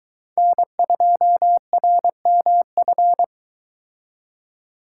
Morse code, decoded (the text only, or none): N2RMF